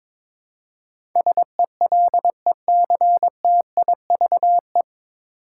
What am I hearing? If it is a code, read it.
Morse code, heard as a keyed tone, about 22 wpm, 700 Hz. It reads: SELECTIVE
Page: Morse Code Ninja practice 22wpm